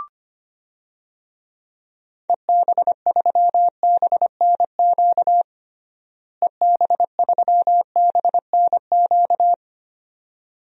Morse code, heard as a keyed tone, about 25 wpm, 700 Hz. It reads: EB3BNQ EB3BNQ
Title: DXCC-100 Call Signs - 6 Characters - Repeated 2x - 25wpm